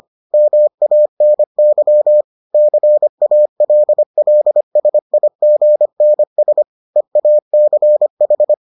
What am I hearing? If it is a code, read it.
Morse code, heard as a keyed tone, about 25 wpm, 600 Hz. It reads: MANY CALLSIGNS EACH